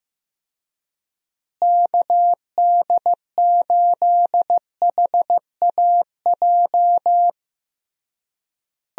Morse code, heard as a keyed tone, about 15 wpm, 700 Hz. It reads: KD8HAJ
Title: USA Call Signs - 6 Characters - Repeated 3x - 15wpm